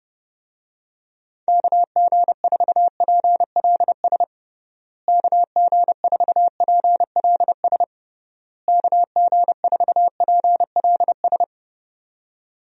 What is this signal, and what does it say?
Morse code, heard as a keyed tone, about 30 wpm, 700 Hz. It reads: KG4PLS KG4PLS KG4PLS
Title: USA Call Signs - 6 Characters - Repeated 3x - 30wpm